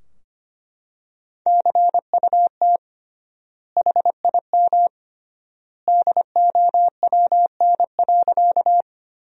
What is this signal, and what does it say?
Morse code, heard as a keyed tone, about 25 wpm, 700 Hz. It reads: CUT HIM DOWN.
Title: 3x Word-Spacing - Sentences from Top 300 Words - 25wpm